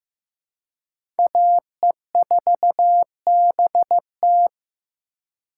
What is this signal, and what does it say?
Morse code, heard as a keyed tone, about 15 wpm, 700 Hz. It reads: AE4BT
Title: USA Call Signs - 5 Characters - 2x Character Spacing - 15wpm